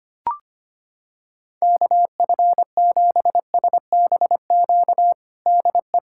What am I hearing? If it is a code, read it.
Morse code, heard as a keyed tone, about 25 wpm, 700 Hz. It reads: KF7SBQ DE
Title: USA Callsign DE Callsign 25wpm